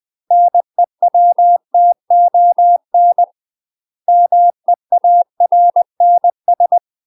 Morse code, heard as a keyed tone, about 20 wpm, 700 Hz. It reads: NEWTON MEARNS
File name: UK MASTER QSOs TRAINING / 20WPM / UK MASTER QSOs 0063